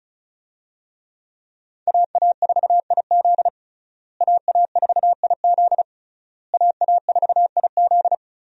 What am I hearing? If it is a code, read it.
Morse code, heard as a keyed tone, about 36 wpm, 700 Hz. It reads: AA4IZ AA4IZ AA4IZ
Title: USA Call Signs - 5 Characters - Repeated 3x - 35wpm